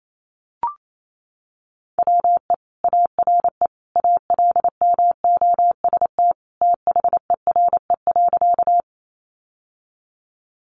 Morse code, heard as a keyed tone, about 28 wpm, 700 Hz. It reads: WE ARE ALMOST THERE.
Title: Sentences from Top 700 Words 28wpm